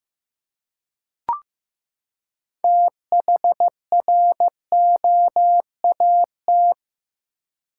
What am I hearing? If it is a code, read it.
Morse code, heard as a keyed tone, about 15 wpm, 700 Hz. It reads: THROAT
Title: Six Letter Words - 15wpm